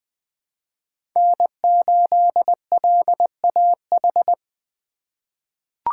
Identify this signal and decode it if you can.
Morse code, heard as a keyed tone, about 20 wpm, 700 Hz. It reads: N8LAH
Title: USA Call Signs - 5 Characters - 3x Character Spacing - 20wpm